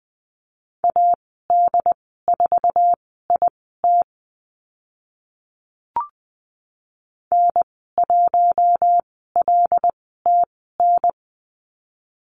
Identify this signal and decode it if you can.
Morse code, heard as a keyed tone, about 20 wpm, spaced as for 15 wpm, 700 Hz. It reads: AD4IT N1LTN
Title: 082 Lesson 14g - Number 2 - Call Signs 20wpm